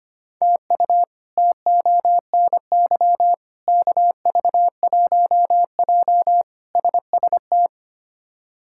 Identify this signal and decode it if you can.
Morse code, heard as a keyed tone, about 25 wpm, 700 Hz. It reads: TU TONY KV1J SST
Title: K1USN SST QSO Practice - 25wpm